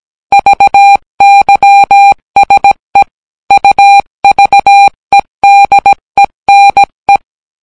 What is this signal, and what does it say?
Morse code, heard as a keyed tone, about 17 wpm, 800 Hz. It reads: VYSE UVEDENE